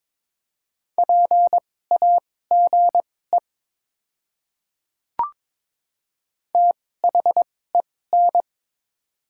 Morse code, heard as a keyed tone, about 22 wpm, spaced as for 16 wpm, 700 Hz. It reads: PAGE THEN
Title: 2x Character Spacing - Top 300 Words - 22wpm